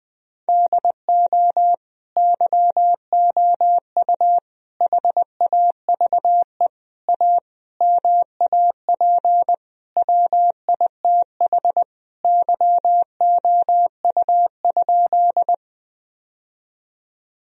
Morse code, heard as a keyed tone, about 21 wpm, 700 Hz. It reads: DO YOU HAVE A MAP WITH YOU?
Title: Sentences from Top 1000 Words - Review Part 2 - 20wpm